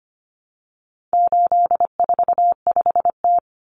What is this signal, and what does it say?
Morse code, heard as a keyed tone, about 25 wpm, 700 Hz. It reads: 845T
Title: Common Cut Numbers - 4-Digit 25wpm